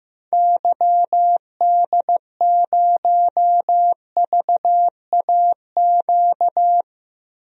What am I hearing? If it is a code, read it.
Morse code, heard as a keyed tone, about 15 wpm, 700 Hz. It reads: YD0VAQ
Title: DXCC-100 Call Signs - 6 Characters - 15wpm